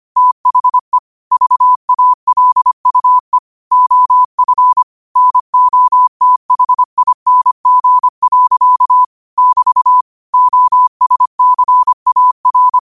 Morse code, heard as a keyed tone, about 25 wpm, 1000 Hz. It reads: THE VALUE OF NOTHING. <BT> OSCAR